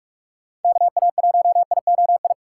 Morse code, heard as a keyed tone, about 43 wpm, 700 Hz. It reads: KA1IOI
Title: USA Call Signs - 6 Characters - 2x Character Spacing - 45wpm